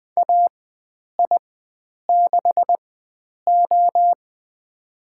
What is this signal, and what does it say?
Morse code, heard as a keyed tone, about 20 wpm, spaced as for 9 wpm, 700 Hz. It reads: AI6O
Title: USA Call Signs - 4 Characters - 4x Character Spacing - 20wpm